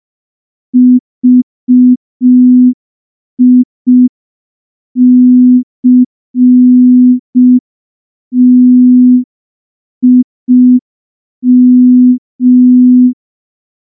Morse code, heard as a keyed tone, about 5 wpm, 250 Hz. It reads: VICTIM
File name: morse / morse-training-05WPM